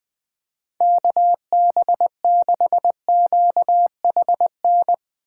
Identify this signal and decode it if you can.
Morse code, heard as a keyed tone, about 20 wpm, 700 Hz. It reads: KB6QHN
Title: Speed Racing - USA 2x3 Call Signs 20wpm